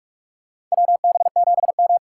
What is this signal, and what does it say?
Morse code, heard as a keyed tone, about 45 wpm, 700 Hz. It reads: WB7M
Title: USA Call Signs - 4 Characters - 3x Character Spacing - 45wpm